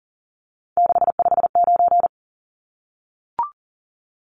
Morse code, heard as a keyed tone, about 40 wpm, 700 Hz. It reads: 659
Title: Numbers - 3-Digit 40wpm